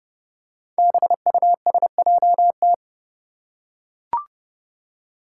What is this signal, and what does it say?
Morse code, heard as a keyed tone, about 30 wpm, 700 Hz. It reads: BUSJT
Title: Five Challenging Letter-Number Combinations 30wpm